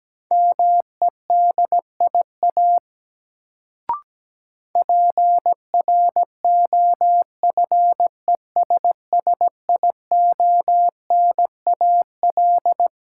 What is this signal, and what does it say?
Morse code, heard as a keyed tone, about 17 wpm, 700 Hz. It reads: MEDIA PROFESSIONAL
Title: Sentences from Top 1000 Words - Review Part 4 - New Word Drill - 17wpm